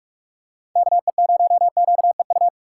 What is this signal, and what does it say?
Morse code, heard as a keyed tone, about 46 wpm, 700 Hz. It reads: KE0QEU
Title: USA Call Signs - 6 Characters - 6x Character Spacing - 45wpm